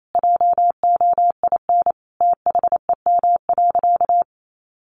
Morse code, heard as a keyed tone, about 28 wpm, 700 Hz. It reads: JOIN THEM.
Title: Sentences from Top 1400 Words 28wpm